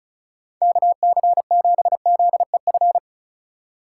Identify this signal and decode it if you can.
Morse code, heard as a keyed tone, about 34 wpm, 700 Hz. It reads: KC7ZEF
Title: USA Call Signs - 6 Characters - 3x Character Spacing - 35wpm